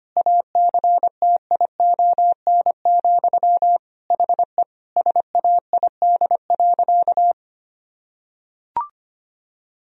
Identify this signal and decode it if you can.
Morse code, heard as a keyed tone, about 25 wpm, 700 Hz. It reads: ACTION, HE SAID.
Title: Sentences from Top 1000 Words - Review Part 4 - 25wpm